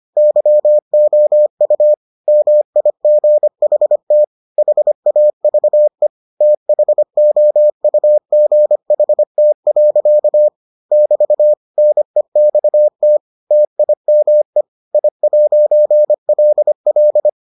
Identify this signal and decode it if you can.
Morse code, heard as a keyed tone, about 25 wpm, 600 Hz. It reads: YOU MIGHT HAVE THOUGHT. <BT> NEXT TIME I'LL